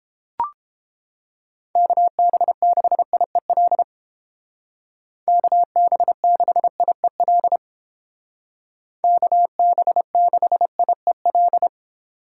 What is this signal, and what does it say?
Morse code, heard as a keyed tone, about 29 wpm, 700 Hz. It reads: KB6IEL KB6IEL KB6IEL